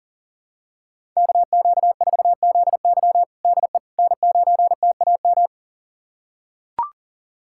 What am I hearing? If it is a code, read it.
Morse code, heard as a keyed tone, about 40 wpm, 700 Hz. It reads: KQ4ZY DE N9TAM
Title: USA Callsign DE Callsign - 5 Characters - 40wpm